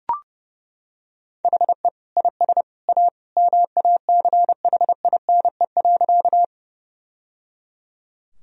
Morse code, heard as a keyed tone, about 30 wpm, 700 Hz. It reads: HE IS A MACHINE.
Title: Sentences from Top 1000 Words - Review Part 2 - 30wpm